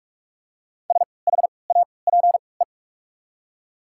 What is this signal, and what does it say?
Morse code, heard as a keyed tone, about 46 wpm, spaced as for 26 wpm, 700 Hz. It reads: SHAPE
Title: 3x Character Spacing - Top 500 Words - 45wpm